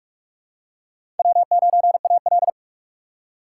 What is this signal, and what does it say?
Morse code, heard as a keyed tone, about 48 wpm, 700 Hz. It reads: W9AL
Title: USA Call Signs - 4x Character Spacing - 45wpm